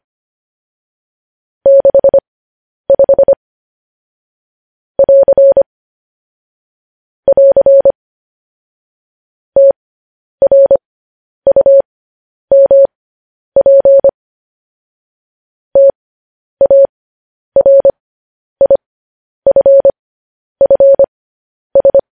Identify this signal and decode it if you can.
Morse code, heard as a keyed tone, about 25 wpm, spaced as for 10 wpm, 550 Hz. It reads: 65 <AR> <AR> TRUMP TARIFFS
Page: news headlines in morse code at 10 wpm